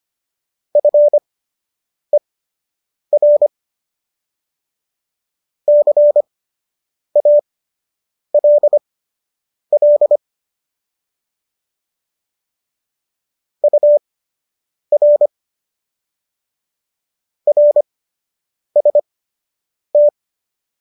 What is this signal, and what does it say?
Morse code, heard as a keyed tone, about 25 wpm, spaced as for 8 wpm, 590 Hz. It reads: FER CALL UR RST